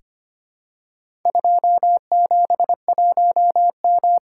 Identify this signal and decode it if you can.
Morse code, heard as a keyed tone, about 25 wpm, 700 Hz. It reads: 271M
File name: practice / 25wpm / Four Letter-Number Combinations - 25wpm